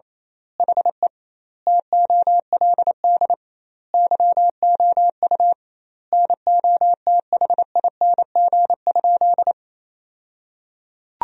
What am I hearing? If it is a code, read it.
Morse code, heard as a keyed tone, about 27 wpm, 700 Hz. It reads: HE TOLD YOU NOTHING?
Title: 2x Word-Spacing - Sentences from Top 500 Words - 28wpm